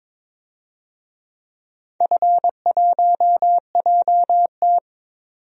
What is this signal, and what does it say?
Morse code, heard as a keyed tone, about 21 wpm, 700 Hz. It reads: F1JT
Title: DXCC-100 Call Signs - 4 Characters - 22wpm